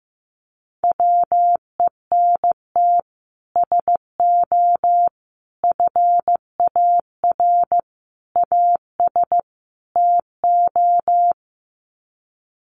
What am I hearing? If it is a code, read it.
Morse code, heard as a keyed tone, about 15 wpm, 700 Hz. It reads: WENT SO FAR AS TO
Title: Sets of 5 Words Encore 15wpm